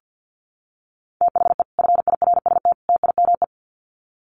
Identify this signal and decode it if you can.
Morse code, heard as a keyed tone, about 50 wpm, 700 Hz. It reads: THE FIRST TIME
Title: Sets of 3 Words 50wpm